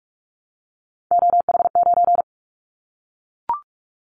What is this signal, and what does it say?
Morse code, heard as a keyed tone, about 44 wpm, 700 Hz. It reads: OH9
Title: Callsign Prefixes for Top 25 DXCC - Repeated 3x - 45wpm